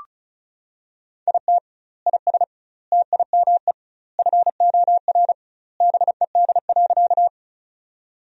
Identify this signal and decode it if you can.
Morse code, heard as a keyed tone, about 36 wpm, 700 Hz. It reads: IT IS TIME FOR BED.
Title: 2x Word-Spacing - Sentences from Top 500 Words - 35wpm